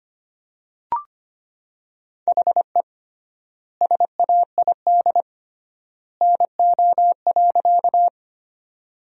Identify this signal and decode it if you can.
Morse code, heard as a keyed tone, about 25 wpm, 700 Hz. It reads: HE SAID NO.
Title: 3x Word-Spacing - Sentences from Top 100 Words - 25wpm